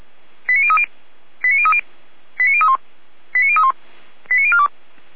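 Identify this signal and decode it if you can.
Five-tone selective call (ZVEI), heard as quick runs of five tones, tones of about 70 ms. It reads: 89030 89030 89031 89031 89042
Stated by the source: Five-tone ZVEI selective call received on an SDR.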